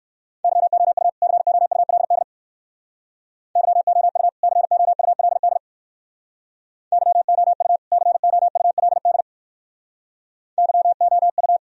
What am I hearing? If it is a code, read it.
Morse code, heard as a keyed tone, about 56 wpm, 700 Hz. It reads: YOU COULD YOU COULD YOU COULD YOU
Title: Speed-Racing - Sets of 2 Words 45wpm